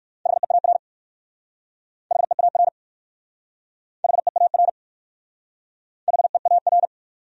Morse code, heard as a keyed tone, about 59 wpm, 700 Hz. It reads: HEAR HEAR HEAR HEAR